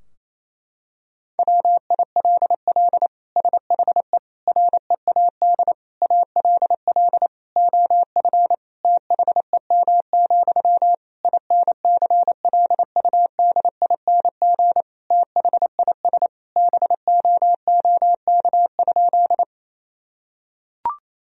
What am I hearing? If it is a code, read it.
Morse code, heard as a keyed tone, about 28 wpm, 700 Hz. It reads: WILL SHE READ ALL OF THEM, INCLUDING THIS BOOK?
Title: Sentences from Top 1000 Words - Review Part 4 - 28wpm